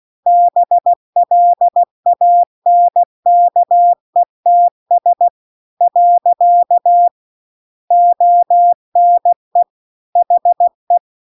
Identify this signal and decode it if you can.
Morse code, heard as a keyed tone, about 16 wpm, 700 Hz. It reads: BLANKETS . ONE HE